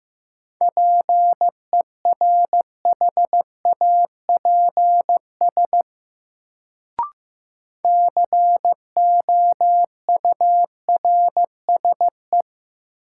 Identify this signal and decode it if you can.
Morse code, heard as a keyed tone, about 15 wpm, 700 Hz. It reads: PERHAPS COURSE
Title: Sentences from Top 500 Words - New Word Drill - 15wpm